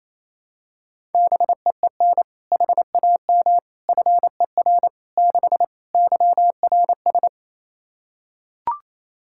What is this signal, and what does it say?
Morse code, heard as a keyed tone, about 28 wpm, 700 Hz. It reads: BEEN HAM FER 6 YRS
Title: Practice QSO - Exchange 3 - More Info - 28wpm